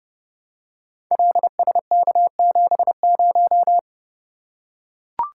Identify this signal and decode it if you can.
Morse code, heard as a keyed tone, about 30 wpm, 700 Hz. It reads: LSK70
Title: Code Groups - 5 Letter:Numbers - 30wpm